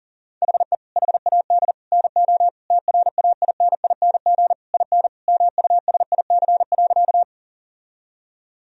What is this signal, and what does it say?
Morse code, heard as a keyed tone, about 40 wpm, 700 Hz. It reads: HE HAD NO TRAINING IN MUSIC.
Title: Sentences from Top 1000 Words - Review Part 4 - 40wpm